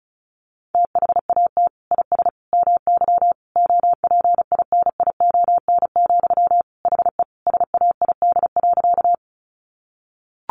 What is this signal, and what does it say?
Morse code, heard as a keyed tone, about 34 wpm, 700 Hz. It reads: THAT IS MY OPINION, HE SAID.
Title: Sentences from Top 1200 Words 35wpm